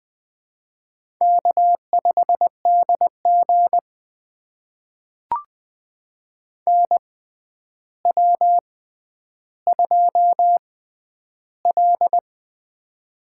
Morse code, heard as a keyed tone, about 20 wpm, spaced as for 10 wpm, 700 Hz. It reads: K5DG NW2L
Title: USA Call Signs - 6x Character Spacing - 20wpm